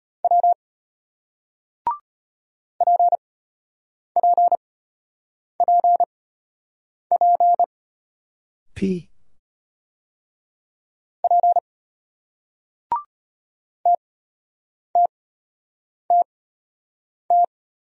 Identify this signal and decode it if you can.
Morse code, heard as a keyed tone, about 34 wpm, spaced as for 8 wpm, 700 Hz. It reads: W PPPP P TTTT